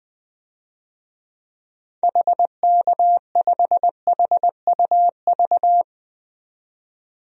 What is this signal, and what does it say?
Morse code, heard as a keyed tone, about 20 wpm, 700 Hz. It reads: HK5HUV
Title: DXCC-100 Call Signs - 6 Characters - 20wpm